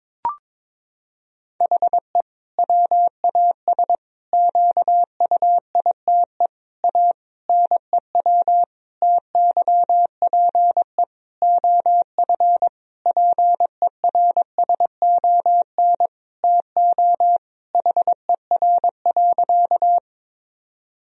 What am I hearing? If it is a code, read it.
Morse code, heard as a keyed tone, about 22 wpm, 700 Hz. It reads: HE WAS QUITE A NEW TYPE OF PERSON TO HER.
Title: Sentences from Top 1000 Words - Review Part 3 - 22wpm